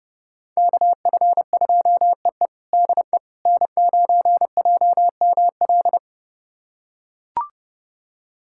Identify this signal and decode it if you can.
Morse code, heard as a keyed tone, about 32 wpm, 700 Hz. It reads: KF2EE DE N9JML
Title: USA Callsign DE Callsign - 5 Characters - 30wpm